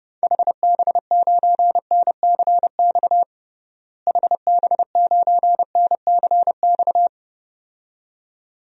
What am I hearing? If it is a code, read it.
Morse code, heard as a keyed tone, about 32 wpm, 700 Hz. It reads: HB9NCX HB9NCX
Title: DXCC-100 Call Signs - 6 Characters - Repeated 2x - 30wpm